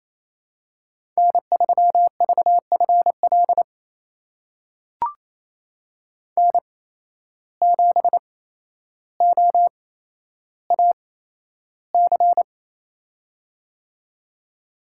Morse code, heard as a keyed tone, about 27 wpm, spaced as for 12 wpm, 700 Hz. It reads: N3VFL N7OAC
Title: USA Call Signs - 5 Characters - 8x Character Spacing - 28wpm